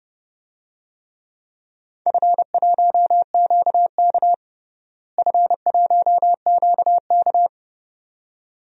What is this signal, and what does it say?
Morse code, heard as a keyed tone, about 32 wpm, 700 Hz. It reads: F1QK F1QK
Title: DXCC-100 Call Signs - 4 Characters - Repeated 2x - 30wpm